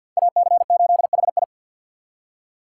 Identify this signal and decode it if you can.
Morse code, heard as a keyed tone, about 46 wpm, 700 Hz. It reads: AC8HI
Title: USA Call Signs - 5 Characters - 2x Character Spacing - 50wpm